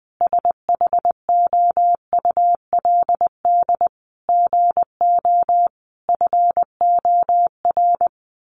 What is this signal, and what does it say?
Morse code, heard as a keyed tone, about 20 wpm, 700 Hz. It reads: SHOULD GO FOR
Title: Sentences from Top 1000 Words 20wpm